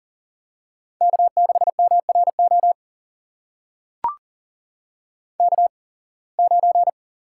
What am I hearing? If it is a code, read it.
Morse code, heard as a keyed tone, about 40 wpm, spaced as for 26 wpm, 700 Hz. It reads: K6MRO K9
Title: USA Call Signs - 8x Character Spacing - 40wpm